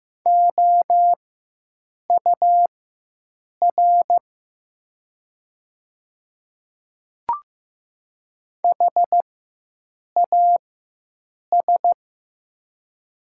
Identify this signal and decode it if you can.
Morse code, heard as a keyed tone, about 15 wpm, spaced as for 7 wpm, 700 Hz. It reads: OUR HAS
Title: 4x Character Spacing - Top 300 Words - 15wpm